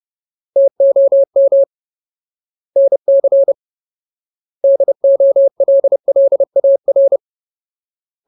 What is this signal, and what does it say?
Morse code, heard as a keyed tone, about 30 wpm, 550 Hz. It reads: TOM NC DOLLAR
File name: Ultimate Word List / 30WPM Ultimate